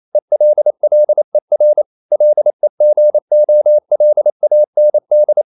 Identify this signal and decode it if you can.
Morse code, heard as a keyed tone, about 28 wpm, 600 Hz. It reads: ELLER LEGOLAND
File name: QRQ 28wpm